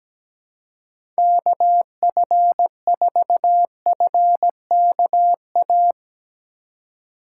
Morse code, heard as a keyed tone, about 17 wpm, 700 Hz. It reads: KF4FKA
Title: USA Call Signs - 2x Character Spacing - 17wpm